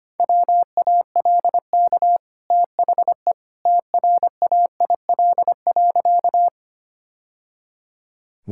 Morse code, heard as a keyed tone, about 25 wpm, 690 Hz. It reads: WALK THE TRAIL.
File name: Easy Sentences with Hints - 25wpm